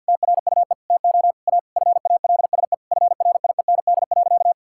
Morse code, heard as a keyed tone, about 50 wpm, 700 Hz. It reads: TRUE TO A FALSE FRIEND.
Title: Sentences from Top 1600 Words 50wpm